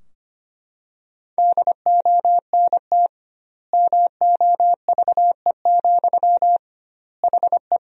Morse code, heard as a keyed tone, about 25 wpm, 700 Hz. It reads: DONT MOVE, HE
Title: 2x Word-Spacing - Sentences from Top 300 Words - 25wpm